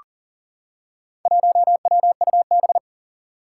Morse code, heard as a keyed tone, about 40 wpm, 700 Hz. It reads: 1WUB
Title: Four Challenging Letter-Number Combinations 40wpm